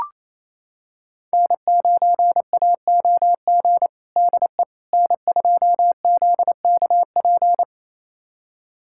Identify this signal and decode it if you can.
Morse code, heard as a keyed tone, about 28 wpm, 700 Hz. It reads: N9AOG DE N2ZKP
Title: USA Callsign DE Callsign - 5 Characters - 28wpm